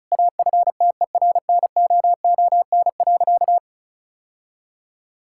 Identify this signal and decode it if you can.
Morse code, heard as a keyed tone, about 36 wpm, 700 Hz. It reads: AFTERNOON.
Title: Sentences from Top 1600 Words 35wpm